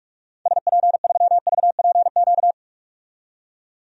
Morse code, heard as a keyed tone, about 46 wpm, 700 Hz. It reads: SP3VPQ
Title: DXCC-100 Call Signs - 6 Characters - 45wpm